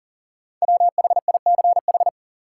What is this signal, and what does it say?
Morse code, heard as a keyed tone, about 40 wpm, 700 Hz. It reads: WHICH
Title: Sentences from Top 100 Words - New Word Drill - 40wpm